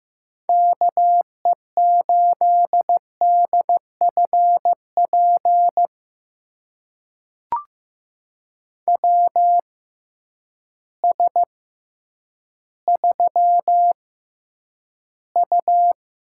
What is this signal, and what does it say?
Morse code, heard as a keyed tone, about 15 wpm, spaced as for 9 wpm, 700 Hz. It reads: KE8DFP WS3U
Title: USA Call Signs - 6x Character Spacing - 15wpm